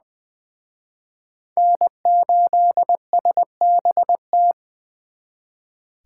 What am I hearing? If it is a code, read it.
Morse code, heard as a keyed tone, about 24 wpm, 700 Hz. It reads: N8SBT
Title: Speed Racing - USA Call Signs 20wpm